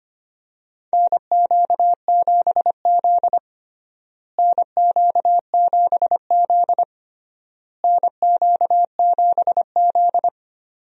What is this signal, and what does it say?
Morse code, heard as a keyed tone, about 25 wpm, 700 Hz. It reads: NQ7Z NQ7Z NQ7Z
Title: USA Call Signs - 4 Characters - Repeated 3x - 25wpm